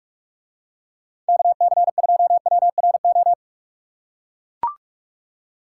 Morse code, heard as a keyed tone, about 45 wpm, 700 Hz. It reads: KC2WRO